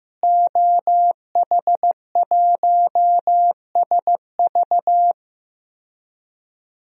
Morse code, heard as a keyed tone, about 15 wpm, 700 Hz. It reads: OH1SV
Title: DXCC-100 Call Signs - 5 Characters - Repeated 2x - 15wpm